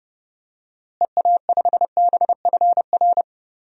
Morse code, heard as a keyed tone, about 30 wpm, 700 Hz. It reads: EA5BFR
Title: DXCC-100 Call Signs - 6 Characters - Repeated 3x - 30wpm